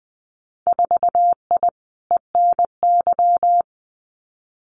Morse code, heard as a keyed tone, about 20 wpm, 700 Hz. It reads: 4I ENY
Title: Winter Field Day Exchanges Encore 20wpm